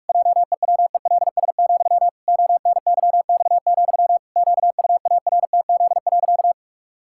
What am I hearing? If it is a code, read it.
Morse code, heard as a keyed tone, about 46 wpm, 700 Hz. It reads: JEWELS, ONYX, QUARTZ.
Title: Pangram Challenge v2 - 45wpm